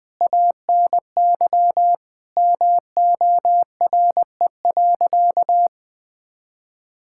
Morse code, heard as a keyed tone, about 21 wpm, 700 Hz. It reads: ANY MORE.
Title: Sentences from Top 1000 Words - Review Part 5 - 20wpm